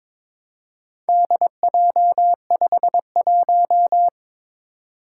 Morse code, heard as a keyed tone, about 21 wpm, 700 Hz. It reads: DJ51
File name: practice / 22wpm / Four Letter-Number Combinations - 22wpm